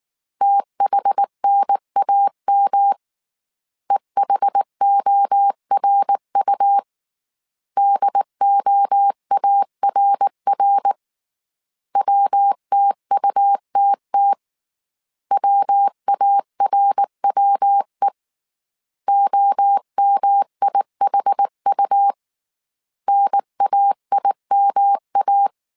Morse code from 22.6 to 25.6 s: NAIMA